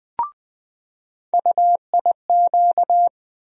Morse code, heard as a keyed tone, about 20 wpm, 700 Hz. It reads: UIQ
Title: USA Call Sign Postfixes - 3 Letters 20wpm